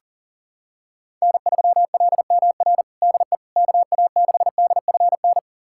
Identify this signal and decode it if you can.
Morse code, heard as a keyed tone, about 37 wpm, 700 Hz. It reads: N3LMR DE KA6DFN
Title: USA Callsign DE Callsign 40wpm